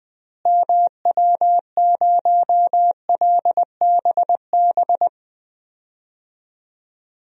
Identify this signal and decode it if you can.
Morse code, heard as a keyed tone, about 21 wpm, 700 Hz. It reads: MW0LBB
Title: DXCC-100 Call Signs - 6 Characters - Repeated 2x - 20wpm